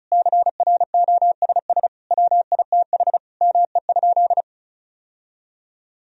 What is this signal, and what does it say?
Morse code, heard as a keyed tone, about 36 wpm, 700 Hz. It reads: CROSS WITH ME?
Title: Sentences from Top 1000 Words - Review Part 1 - 35wpm